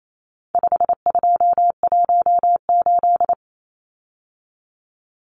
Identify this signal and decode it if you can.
Morse code, heard as a keyed tone, about 28 wpm, 700 Hz. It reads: 5218
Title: Numbers - 4-Digit 28wpm